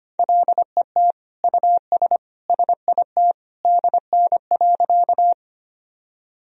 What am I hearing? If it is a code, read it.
Morse code, heard as a keyed tone, about 25 wpm, 700 Hz. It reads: LET US SIT DN.
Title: Sentences from Top 500 Words and Abbreviations 25wpm